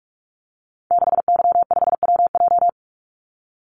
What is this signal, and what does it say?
Morse code, heard as a keyed tone, about 44 wpm, 700 Hz. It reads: 6Y5WJ
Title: Call Signs Repeated 3x - 45wpm